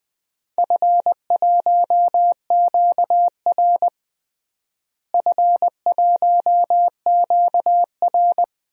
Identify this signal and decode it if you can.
Morse code, heard as a keyed tone, about 21 wpm, 700 Hz. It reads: F1QR F1QR
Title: DXCC-100 Call Signs - 4 Characters - Repeated 3x - 20wpm